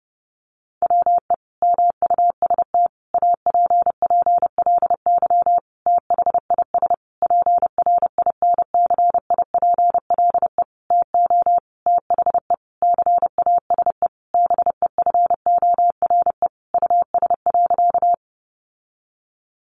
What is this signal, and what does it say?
Morse code, heard as a keyed tone, about 30 wpm, 700 Hz. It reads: WE MUST APPLY THIS PRINCIPLE TO THE CASE BEFORE US.
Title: Sentences from Top 1400 Words 30wpm